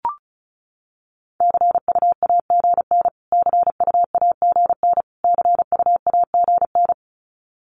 Morse code, heard as a keyed tone, about 34 wpm, 700 Hz. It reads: CUAGN CUAGN CUAGN
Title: Abbreviations Repeated 3x - 35wpm